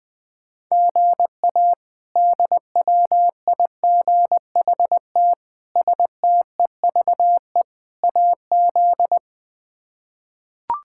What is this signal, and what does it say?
Morse code, heard as a keyed tone, about 20 wpm, 700 Hz. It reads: GA DWIGHT STEVE AZ
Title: K1USN SST QSO Practice - 20wpm